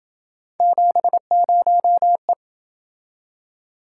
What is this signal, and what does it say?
Morse code, heard as a keyed tone, about 27 wpm, 700 Hz. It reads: 70E